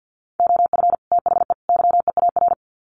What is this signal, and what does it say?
Morse code, heard as a keyed tone, about 50 wpm, 700 Hz. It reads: OF THE YEAR
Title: Sets of 3 Words 50wpm